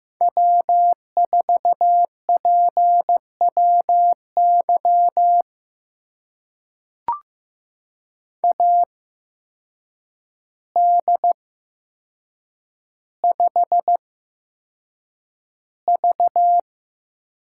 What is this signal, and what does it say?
Morse code, heard as a keyed tone, about 15 wpm, spaced as for 7 wpm, 700 Hz. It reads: W4PWY AD5V
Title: USA Call Signs - 5 Characters - 8x Character Spacing - 15wpm